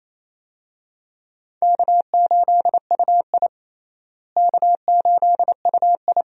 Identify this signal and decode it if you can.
Morse code, heard as a keyed tone, about 28 wpm, 700 Hz. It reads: K8UI K8UI
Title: USA Call Signs - 4 Characters - Repeated 3x - 28wpm